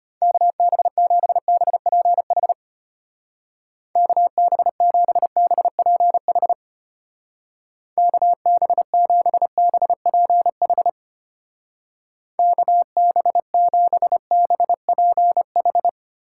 Morse code, heard as a keyed tone, about 31 wpm, 700 Hz. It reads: KB7BPH KB7BPH KB7BPH KB7BPH